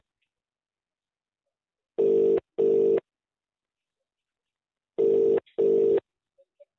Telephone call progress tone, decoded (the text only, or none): ringback tone